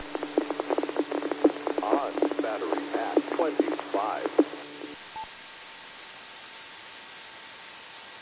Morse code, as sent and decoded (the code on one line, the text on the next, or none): none